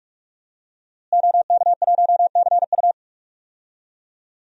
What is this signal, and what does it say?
Morse code, heard as a keyed tone, about 41 wpm, 700 Hz. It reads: OK1CU
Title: DXCC-100 Call Signs - 5 Characters - Repeated 2x - 45wpm